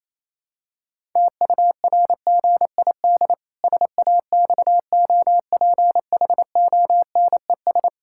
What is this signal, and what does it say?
Morse code, heard as a keyed tone, about 28 wpm, 700 Hz. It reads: TURGID SAXOPHONES
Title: Pangram Challenge v2 - 28wpm